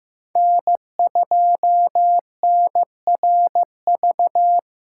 Morse code, heard as a keyed tone, about 15 wpm, 700 Hz. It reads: N2NRV
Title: USA Callsign DE Callsign - 5 Characters - 15wpm